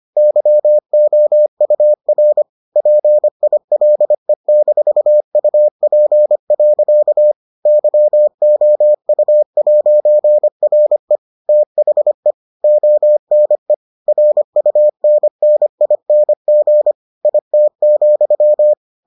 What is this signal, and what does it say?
Morse code, heard as a keyed tone, about 25 wpm, 600 Hz. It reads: YOUR PILE-UP. YOU'RE THE ONE RUNNING IT,